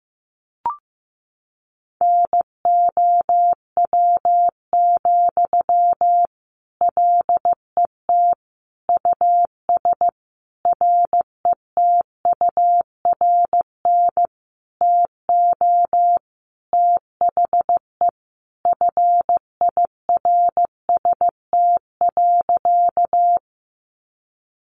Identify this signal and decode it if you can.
Morse code, heard as a keyed tone, about 15 wpm, 700 Hz. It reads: NOW, LET US RETURN TO THE FIRST.
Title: Sentences from Top 800 Words 15wpm